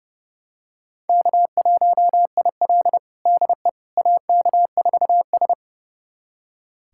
Morse code, heard as a keyed tone, about 30 wpm, 700 Hz. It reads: K1IL DE AK4S
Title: USA Callsign DE Callsign - 4 Characters - 30wpm